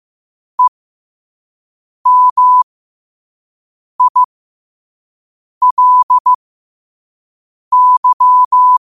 Morse code, heard as a keyed tone, about 15 wpm, spaced as for 5 wpm, 1000 Hz. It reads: EMILY
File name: qotdfiles / 5wpm